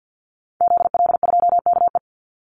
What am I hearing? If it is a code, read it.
Morse code, heard as a keyed tone, about 50 wpm, 700 Hz. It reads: ZL2KE
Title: Call Signs - 50wpm